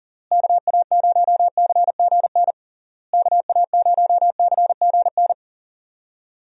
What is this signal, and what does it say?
Morse code, heard as a keyed tone, about 40 wpm, 700 Hz. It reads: KA0CGN KA0CGN
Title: USA Call Signs - 6 Characters - Repeated 3x - 40wpm